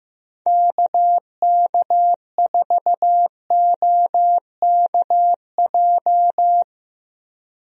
Morse code, heard as a keyed tone, about 16 wpm, 700 Hz. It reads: KK4OKJ
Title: USA Call Signs - 6 Characters - Repeated 2x - 15wpm